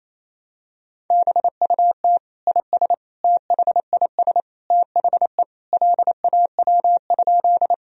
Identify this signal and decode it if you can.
Morse code, heard as a keyed tone, about 28 wpm, 700 Hz. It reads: BUT IS THIS THE LAW?
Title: Sentences from Top 1000 Words - Review Part 3 - 28wpm